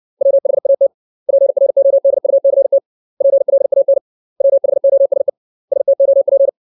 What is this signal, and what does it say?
Morse code, heard as a keyed tone, about 60 wpm, 550 Hz. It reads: WHAT PRODUCT PLAN WHOSE STOP
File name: 632 words / 632 Words 60WPM